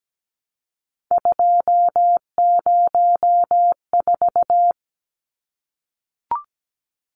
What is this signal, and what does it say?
Morse code, heard as a keyed tone, about 17 wpm, 700 Hz. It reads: 204
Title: Numbers - 3-Digit 17wpm